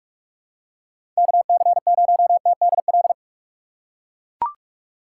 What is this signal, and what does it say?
Morse code, heard as a keyed tone, about 42 wpm, 700 Hz. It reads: KC0TDL
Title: USA Call Signs - 3x Character Spacing - 45wpm